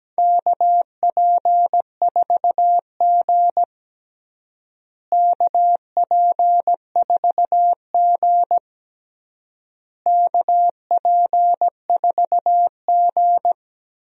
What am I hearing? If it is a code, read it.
Morse code, heard as a keyed tone, about 17 wpm, 700 Hz. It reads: KP4G KP4G KP4G
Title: DXCC-100 Call Signs - 4 Characters - Repeated 3x - 17wpm